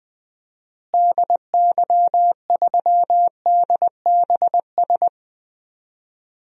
Morse code, heard as a keyed tone, about 20 wpm, 700 Hz. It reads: DY3DBS
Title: DXCC-100 Call Signs - 6 Characters - Repeated 2x - 20wpm